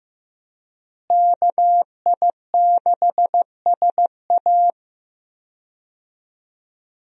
Morse code, heard as a keyed tone, about 15 wpm, 700 Hz. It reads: KI6SA
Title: USA Call Signs - 5 Characters - Repeated 2x - 15wpm